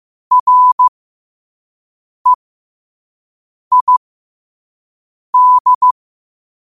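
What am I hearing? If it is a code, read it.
Morse code, heard as a keyed tone, about 15 wpm, spaced as for 5 wpm, 1000 Hz. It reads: REID
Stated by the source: Morse code practice at 5 wpm.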